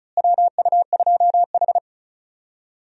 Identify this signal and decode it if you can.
Morse code, heard as a keyed tone, about 34 wpm, 700 Hz. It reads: WU2H
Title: USA Call Signs - 4 Characters - Repeated 3x - 35wpm